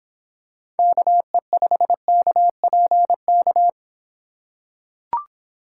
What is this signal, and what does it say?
Morse code, heard as a keyed tone, about 26 wpm, 700 Hz. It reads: KE5KPK